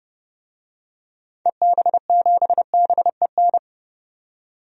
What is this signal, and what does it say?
Morse code, heard as a keyed tone, about 30 wpm, 700 Hz. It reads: EB7BEN
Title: DXCC-100 Call Signs - 6 Characters - 30wpm